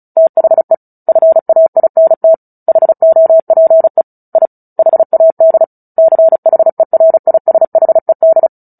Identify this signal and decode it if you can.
Morse code, heard as a keyed tone, about 34 wpm, 650 Hz. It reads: THE FAINT HOPE I HAD CHERISHED
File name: Book Two / 09-Chapter 9 - 35 WPM